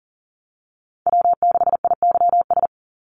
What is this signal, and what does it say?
Morse code, heard as a keyed tone, about 40 wpm, 700 Hz. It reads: W6IYS
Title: NA SOTA Chaser Callsigns 40wpm